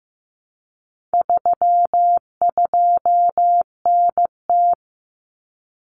Morse code, heard as a keyed tone, about 15 wpm, 700 Hz. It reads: 32NT
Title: Common Cut Numbers - 4-Digit 15wpm